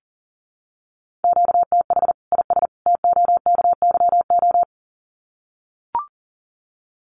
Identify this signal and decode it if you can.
Morse code, heard as a keyed tone, about 40 wpm, 700 Hz. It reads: QTH IS TOKYO
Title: Common QSO Phrases 40wpm